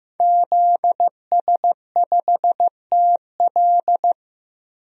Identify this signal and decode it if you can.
Morse code, heard as a keyed tone, about 15 wpm, 700 Hz. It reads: ZS5TL
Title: DXCC-100 Call Signs - 5 Characters - 15wpm